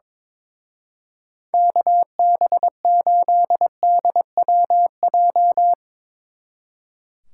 Morse code, heard as a keyed tone, about 24 wpm, 700 Hz. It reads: KB8DWJ
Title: Speed Racing - USA 2x3 Call Signs 22wpm